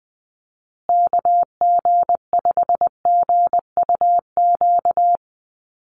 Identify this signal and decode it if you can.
Morse code, heard as a keyed tone, about 19 wpm, 700 Hz. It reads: KG5GUQ
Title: USA Call Signs Repeated 3x 20wpm